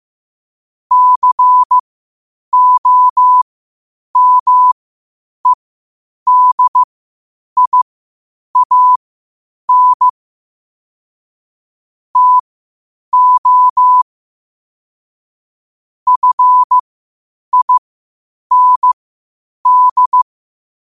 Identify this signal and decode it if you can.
Morse code, heard as a keyed tone, about 15 wpm, spaced as for 8 wpm, 1000 Hz. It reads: COMEDIAN TO FIND
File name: qotdfiles / 8wpm